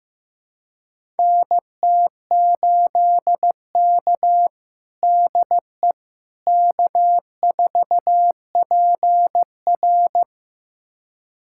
Morse code, heard as a keyed tone, about 16 wpm, 700 Hz. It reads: NT8K DE K4PR
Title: USA Callsign DE Callsign - 4 Characters - 15wpm